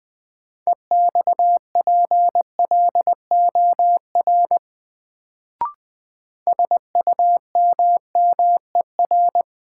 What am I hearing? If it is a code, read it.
Morse code, heard as a keyed tone, about 20 wpm, 700 Hz. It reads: EXPLOR SUMMER
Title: Sentences from Top 1100 Words - New Word Drill Alt - 20wpm